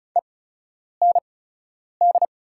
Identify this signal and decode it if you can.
Morse code, heard as a keyed tone, about 36 wpm, spaced as for 10 wpm, 690 Hz. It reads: END
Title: 8x Character Spacing - Top 200 Words - 35wpm